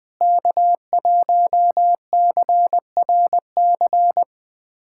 Morse code, heard as a keyed tone, about 20 wpm, 700 Hz. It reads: K1CRC
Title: USA Callsign DE Callsign - 5 Characters - 20wpm